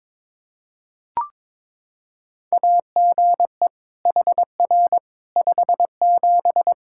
Morse code, heard as a keyed tone, about 22 wpm, 700 Hz. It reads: AGE HR 57
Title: Practice QSO - Exchange 3 - More Info - 22wpm